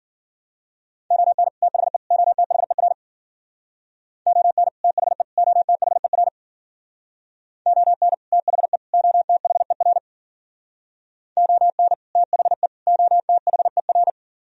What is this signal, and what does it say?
Morse code, heard as a keyed tone, about 50 wpm, 700 Hz. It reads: ON THE OTHER ON THE OTHER ON THE OTHER ON THE OTHER
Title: Speed-Racing - Sets of 3 Words 40wpm